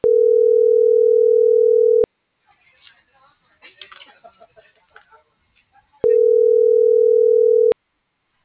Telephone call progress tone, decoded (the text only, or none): ringback tone